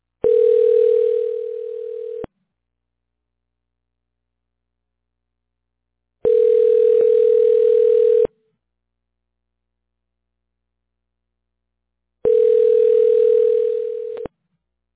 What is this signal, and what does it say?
Ringback tone (North America), 440 + 480 Hz.